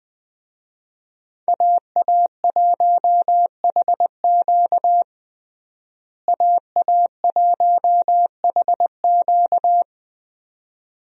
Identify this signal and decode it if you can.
Morse code, heard as a keyed tone, about 20 wpm, 700 Hz. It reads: AA1HQ AA1HQ
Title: USA Call Signs - 5 Characters - Repeated 2x - 20wpm